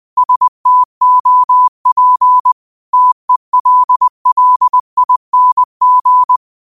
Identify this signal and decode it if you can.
Morse code, heard as a keyed tone, about 20 wpm, 1000 Hz. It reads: STOP TELLING